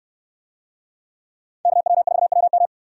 Morse code, heard as a keyed tone, about 68 wpm, 700 Hz. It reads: KW4WM